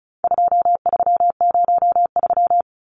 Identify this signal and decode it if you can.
Morse code, heard as a keyed tone, about 34 wpm, 700 Hz. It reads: 2303
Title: Numbers - 4-Digit 35wpm